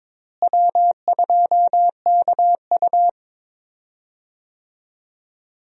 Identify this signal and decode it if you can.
Morse code, heard as a keyed tone, about 22 wpm, 700 Hz. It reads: W2KU
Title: USA Call Signs - 4 Characters - Repeated 3x - 22wpm